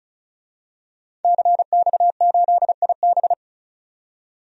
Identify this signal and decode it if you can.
Morse code, heard as a keyed tone, about 36 wpm, 700 Hz. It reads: CX8IB
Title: DXCC-100 Call Signs - 5 Characters - 35wpm